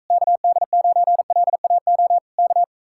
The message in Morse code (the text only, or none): KD9LAO K